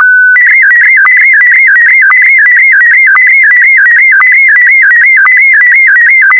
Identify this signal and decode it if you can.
SSTV (slow-scan television), Scottie DX.
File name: SSTV ScottieDX Leo